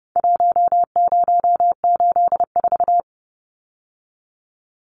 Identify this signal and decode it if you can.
Morse code, heard as a keyed tone, about 30 wpm, 700 Hz. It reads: 1084
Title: Numbers - Contest Style - 4-Digit 30wpm